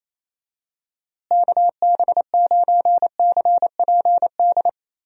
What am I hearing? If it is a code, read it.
Morse code, heard as a keyed tone, about 28 wpm, 700 Hz. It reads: KB9CPD
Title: USA Call Signs - 6 Characters - Repeated 3x - 28wpm